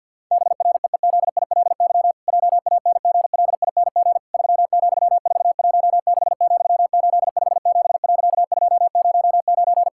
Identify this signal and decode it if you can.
Morse code, heard as a keyed tone, about 50 wpm, 700 Hz. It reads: BREEZILY JANGLING 3,416,857.209
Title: Pangram Challenge v2 - 50wpm